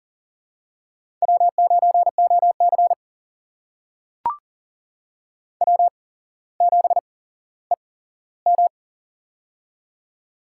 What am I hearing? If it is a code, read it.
Morse code, heard as a keyed tone, about 40 wpm, spaced as for 17 wpm, 700 Hz. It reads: W9OC W7EM
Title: USA Call Signs - 4 Characters - 8x Character Spacing - 40wpm